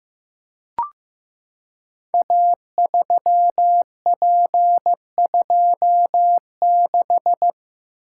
Morse code, heard as a keyed tone, about 15 wpm, 700 Hz. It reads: A3P26
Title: Code Groups - 5 Letter:Numbers - 15wpm